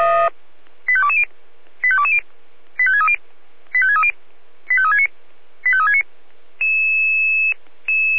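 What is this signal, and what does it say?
Five-tone selective call (ZVEI), heard as quick runs of five tones, tones of about 70 ms. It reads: 85220 85220 85630 85630 85369 85369
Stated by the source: Received with an SDR; ZVEI five-tone selective call.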